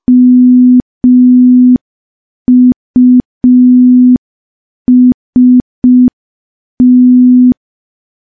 Morse code, heard as a keyed tone, about 5 wpm, 250 Hz. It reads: MUST
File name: morse / morse-training-05WPM